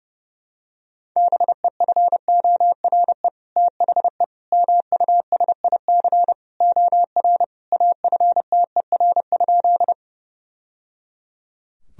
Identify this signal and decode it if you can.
Morse code, heard as a keyed tone, about 30 wpm, 700 Hz. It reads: BEFORE THE MUSIC OR AFTER?
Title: Sentences from Top 1000 Words - Review Part 1 - 30wpm